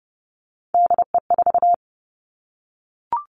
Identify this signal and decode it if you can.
Morse code, heard as a keyed tone, about 30 wpm, 700 Hz. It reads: DE4
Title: Three Challenging Letter-Number Combinations 30wpm